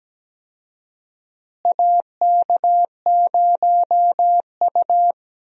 Morse code, heard as a keyed tone, about 17 wpm, 700 Hz. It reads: AK0U
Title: USA Call Signs - 4 Characters - Repeated 3x - 17wpm